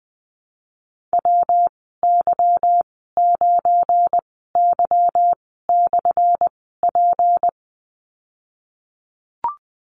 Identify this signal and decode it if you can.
Morse code, heard as a keyed tone, about 20 wpm, spaced as for 15 wpm, 700 Hz. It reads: WY9Y/P
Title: 179 Lesson 29g - Slash - Call Signs 20wpm